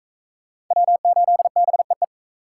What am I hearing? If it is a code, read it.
Morse code, heard as a keyed tone, about 39 wpm, 700 Hz. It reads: W8BEE